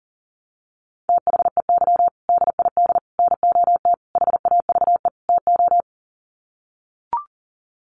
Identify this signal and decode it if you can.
Morse code, heard as a keyed tone, about 40 wpm, 700 Hz. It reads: THEY DID NOT HAVE TO
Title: Sets of 5 Words Encore2 40wpm